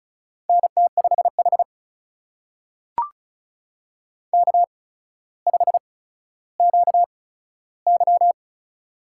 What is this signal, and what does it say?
Morse code, heard as a keyed tone, about 36 wpm, spaced as for 15 wpm, 700 Hz. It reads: NT5H K5QY
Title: USA Call Signs - 8x Character Spacing - 35wpm